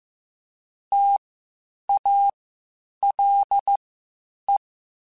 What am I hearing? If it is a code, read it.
Morse code, heard as a keyed tone, about 15 wpm, spaced as for 8 wpm, 800 Hz. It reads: TALE
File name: Spaced Alphas / Spaced Alphas mp3s / Ts sp @13